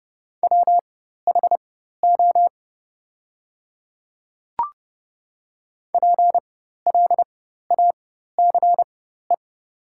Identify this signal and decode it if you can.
Morse code, heard as a keyed tone, about 30 wpm, spaced as for 14 wpm, 700 Hz. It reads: WHO PLACE
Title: 4x Character Spacing - Top 300 Words - 30wpm